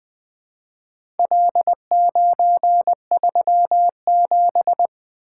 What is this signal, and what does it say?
Morse code, heard as a keyed tone, about 20 wpm, 700 Hz. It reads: L937
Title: Four Letter-Number Combinations - 20wpm